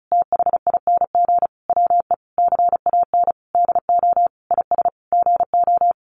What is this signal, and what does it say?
Morse code, heard as a keyed tone, about 34 wpm, 700 Hz. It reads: THING WE CAN DO IS GO
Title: Sentences from Top 200 Words 35wpm